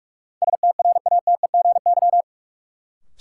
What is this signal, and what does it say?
Morse code, heard as a keyed tone, about 46 wpm, 700 Hz. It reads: STRATEGY
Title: Sentences from Top 1000 Words - Review Part 5 - New Word Drill - 45wpm